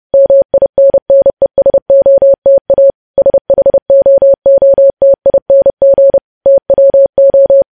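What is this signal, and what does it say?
Morse code, heard as a keyed tone, about 30 wpm, 550 Hz. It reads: MINNESOTA SHOOTING TWO
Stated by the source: Morse code news headlines practice sent at 30 wpm.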